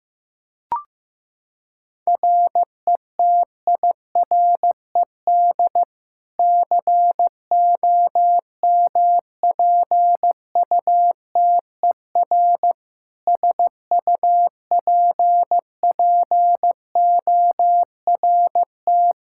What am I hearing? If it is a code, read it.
Morse code, heard as a keyed tone, about 15 wpm, 700 Hz. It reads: RETIRED COMPUTER SUPPORT
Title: Practice QSO - Exchange 3 - More Info - 15wpm